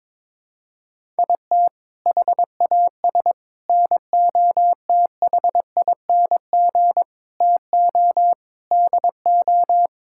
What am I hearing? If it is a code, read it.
Morse code, heard as a keyed tone, about 22 wpm, 700 Hz. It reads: IT HAS NOTHING TO DO
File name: practice / 22wpm / Sentences with 4-Word Phrases - 22wpm